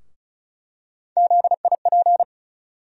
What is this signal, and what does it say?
Morse code, heard as a keyed tone, about 36 wpm, 700 Hz. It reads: ZIP
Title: USA Call Sign Postfixes - 3 Letters 35wpm